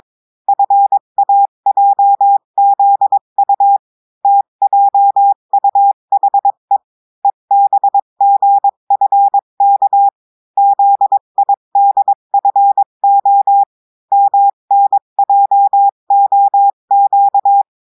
Morse code, heard as a keyed tone, about 22 wpm, 800 Hz. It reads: FAJZU TJUHE EBGFK ZIDFO MNJOQ
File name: cours-du-mercredi / audios / cours-853-du-2-mai-2024-etxte chiffre et texte clair-a-20-wpm